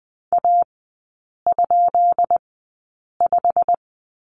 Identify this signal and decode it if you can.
Morse code, heard as a keyed tone, about 20 wpm, spaced as for 8 wpm, 700 Hz. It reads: A?5
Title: 178 Lesson 29e - Slash - Three Characters 20wpm